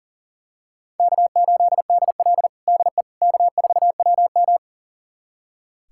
Morse code, heard as a keyed tone, about 40 wpm, 700 Hz. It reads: K8DL DE K4WM
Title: USA Callsign DE Callsign - 4 Characters - 40wpm